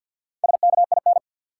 Morse code, heard as a keyed tone, about 48 wpm, 700 Hz. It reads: SKIN
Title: Four Letter Words - 50wpm